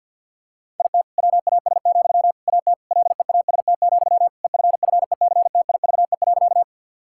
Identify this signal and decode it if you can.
Morse code, heard as a keyed tone, about 50 wpm, 700 Hz. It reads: IT WAS, AT LEAST, EFFECTIVE.
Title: Sentences from Top 1000 Words - Review Part 6 - 50wpm